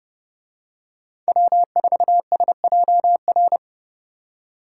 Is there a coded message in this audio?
Yes: Morse code from 1.3 to 3.6 s, heard as a keyed tone, about 30 wpm, 700 Hz. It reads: W4SJR